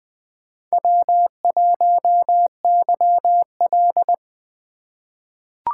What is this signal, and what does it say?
Morse code, heard as a keyed tone, about 20 wpm, 700 Hz. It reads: W1YL
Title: USA Call Signs - 4 Characters - 4x Character Spacing - 20wpm